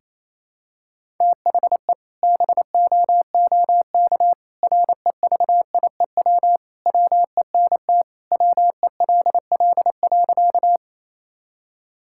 Morse code, heard as a keyed tone, about 28 wpm, 700 Hz. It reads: THE BOOK REVIEW WENT WELL.
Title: Sentences from Top 1000 Words - Review Part 6 - 28wpm